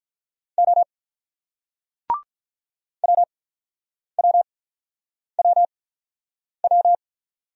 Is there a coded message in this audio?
Yes: Morse code from 0.6 to 7.0 s, heard as a keyed tone, about 46 wpm, spaced as for 9 wpm, 690 Hz. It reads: O WWWW